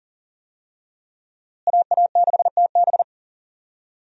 Morse code, heard as a keyed tone, about 38 wpm, 700 Hz. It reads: AA6TB
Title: USA Call Signs - 5 Characters - Repeated 2x - 40wpm